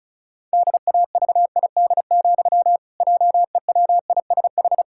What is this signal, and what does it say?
Morse code, heard as a keyed tone, about 36 wpm, 700 Hz. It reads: DAVID, JEWISH
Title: Pangram Challenge v2 - 35wpm